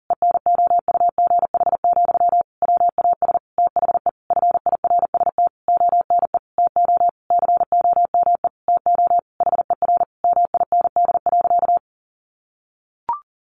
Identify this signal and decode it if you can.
Morse code, heard as a keyed tone, about 40 wpm, 700 Hz. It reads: ENOUGH, WAS THE FIRST ONE TO COME TO HER MIND.
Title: Sentences from Top 1300 Words 40wpm